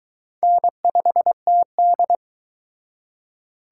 Morse code, heard as a keyed tone, about 23 wpm, 700 Hz. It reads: N5TD